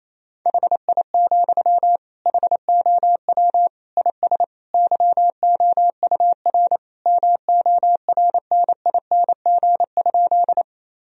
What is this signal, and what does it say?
Morse code, heard as a keyed tone, about 28 wpm, 700 Hz. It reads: HI, HOW IS YOUR MORNING?
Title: Sentences from Top 1000 Words - Review Part 3 - 28wpm